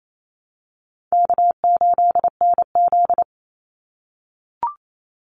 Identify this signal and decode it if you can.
Morse code, heard as a keyed tone, about 28 wpm, 700 Hz. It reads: K8NZ
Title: Call Signs - 28wpm